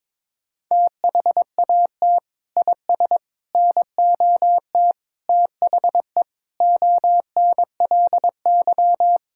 Morse code, heard as a keyed tone, about 22 wpm, 700 Hz. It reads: THAT IS NOT THE ONLY
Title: Sentences from Top 1000 Words - Review Part 5 - 22wpm